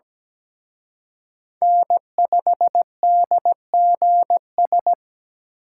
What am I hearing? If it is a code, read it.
Morse code, heard as a keyed tone, about 20 wpm, 700 Hz. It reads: N5DGS
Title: Speed Racing - USA 1x3 Call Signs 17wpm